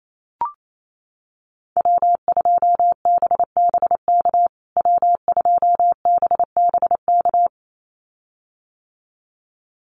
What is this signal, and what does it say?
Morse code, heard as a keyed tone, about 28 wpm, 700 Hz. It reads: W2BBK W2BBK
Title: USA Call Signs Repeated 2x 28wpm